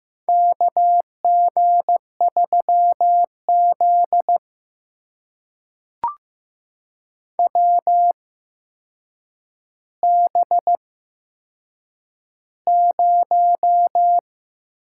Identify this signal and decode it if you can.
Morse code, heard as a keyed tone, about 15 wpm, spaced as for 7 wpm, 700 Hz. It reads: KG3Z WB0
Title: USA Call Signs - 4 Characters - 8x Character Spacing - 15wpm